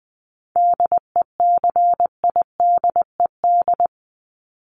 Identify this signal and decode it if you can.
Morse code, heard as a keyed tone, about 20 wpm, 700 Hz. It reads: DECIDED
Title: ED Suffix - Top 100 Words with Base Words 20wpm